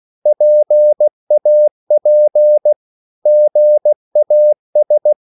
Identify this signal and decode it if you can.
Morse code, heard as a keyed tone, about 16 wpm, 600 Hz. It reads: PAP GAS
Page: Korte danske ord 16wpm